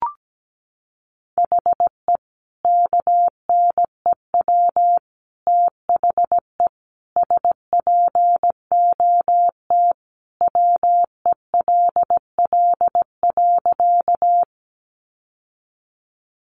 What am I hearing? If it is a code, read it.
Morse code, heard as a keyed tone, about 17 wpm, 700 Hz. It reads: HE KNEW THE SPOT WELL.
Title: Sentences from Top 1100 Words 17wpm